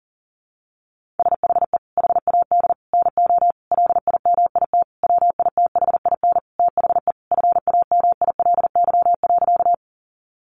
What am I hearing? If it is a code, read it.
Morse code, heard as a keyed tone, about 37 wpm, 700 Hz. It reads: SHE HAD NO LIMIT WITHIN THE FAMILY.
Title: Sentences from Top 1500 Words 40wpm